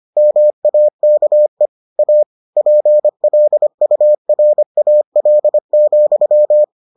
Morse code, heard as a keyed tone, about 25 wpm, 600 Hz. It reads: MAKE A PLURAL,